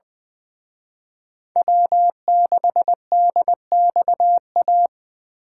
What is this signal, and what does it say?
Morse code, heard as a keyed tone, about 23 wpm, 700 Hz. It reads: W6DXA
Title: Speed Racing - USA Call Signs 20wpm